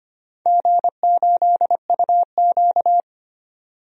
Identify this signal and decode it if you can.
Morse code, heard as a keyed tone, about 25 wpm, 700 Hz. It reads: G8UQ
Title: DXCC-100 Call Signs - 4 Characters - 25wpm